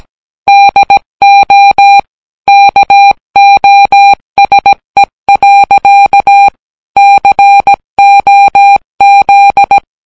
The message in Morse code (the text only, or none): DO KOSE. COZ